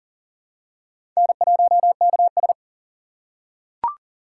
Morse code, heard as a keyed tone, about 40 wpm, 700 Hz. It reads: N1KS
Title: USA Call Signs - 4 Characters - 4x Character Spacing - 40wpm